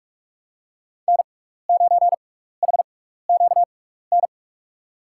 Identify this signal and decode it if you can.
Morse code, heard as a keyed tone, about 44 wpm, spaced as for 15 wpm, 700 Hz. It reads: N9HQN
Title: USA Call Signs - 6x Character Spacing - 45wpm